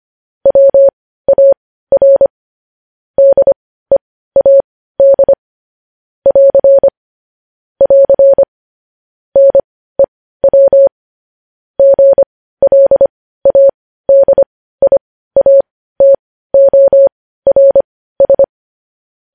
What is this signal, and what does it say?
Morse code, heard as a keyed tone, about 25 wpm, spaced as for 15 wpm, 550 Hz. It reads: WAR DEAD <AR> <AR> NEW GLADIATORS
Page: news headlines in morse code at 15 wpm